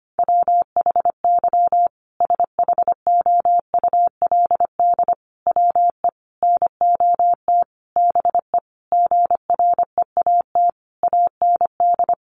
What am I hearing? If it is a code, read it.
Morse code, heard as a keyed tone, about 25 wpm, 700 Hz. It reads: WHY SHOULD WE NOT BE GREAT AND
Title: Sentences from Top 1500 Words 25wpm